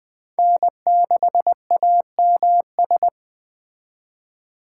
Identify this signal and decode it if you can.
Morse code, heard as a keyed tone, about 23 wpm, 700 Hz. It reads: N6AMS
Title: Speed Racing - USA 1x3 Call Signs 20wpm